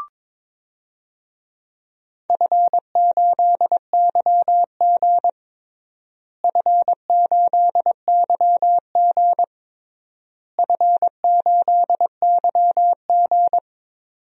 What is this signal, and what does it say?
Morse code, heard as a keyed tone, about 22 wpm, 700 Hz. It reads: F8YG F8YG F8YG
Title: DXCC-100 Call Signs - 4 Characters - Repeated 3x - 22wpm